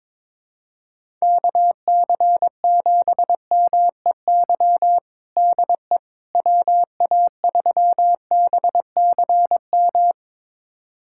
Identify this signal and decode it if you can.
Morse code, heard as a keyed tone, about 22 wpm, 700 Hz. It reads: KC7MEY DE WA3BCM
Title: USA Callsign DE Callsign 22wpm